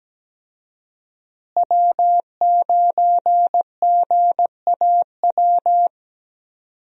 Morse code, heard as a keyed tone, about 17 wpm, 700 Hz. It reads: W9GAW
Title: USA Call Signs - 5 Characters - Repeated 2x - 17wpm